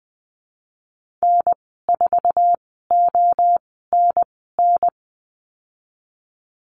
Morse code, heard as a keyed tone, about 20 wpm, spaced as for 15 wpm, 700 Hz. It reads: N4ONN
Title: 055 Lesson 10g - Letter r - Call Signs 20wpm